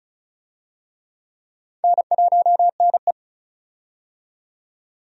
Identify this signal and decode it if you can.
Morse code, heard as a keyed tone, about 36 wpm, 700 Hz. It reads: N1NE
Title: USA Call Signs - 4 Characters - 35wpm